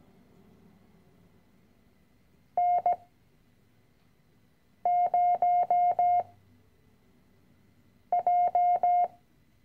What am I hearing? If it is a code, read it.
Morse code, heard as a keyed tone, about 17 wpm, 710 Hz. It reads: N 0 J
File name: CW-Online-Kurs / Anfänger / Lektion 010